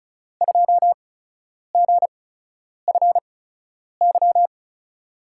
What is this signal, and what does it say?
Morse code, heard as a keyed tone, about 36 wpm, spaced as for 10 wpm, 700 Hz. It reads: 2GFY
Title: USA Call Signs - 8x Character Spacing - 35wpm